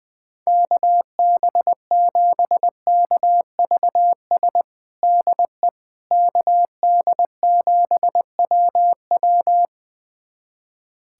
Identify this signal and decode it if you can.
Morse code, heard as a keyed tone, about 20 wpm, 700 Hz. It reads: KB7KVS DE KD7WW
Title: USA Callsign DE Callsign 20wpm